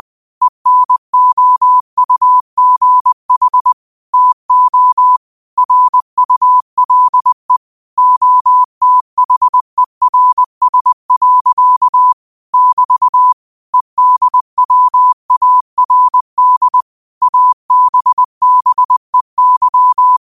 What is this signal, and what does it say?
Morse code, heard as a keyed tone, about 20 wpm, 1000 Hz. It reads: ENOUGH TO RULE OTHERS. <BT> EDWARD ABBEY